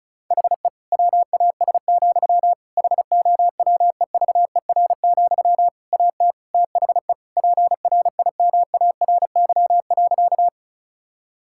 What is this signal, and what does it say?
Morse code, heard as a keyed tone, about 34 wpm, 700 Hz. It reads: HE WAS, HOWEVER, AT THE PRIMARY.
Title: Sentences from Top 1000 Words - Review Part 5 - 35wpm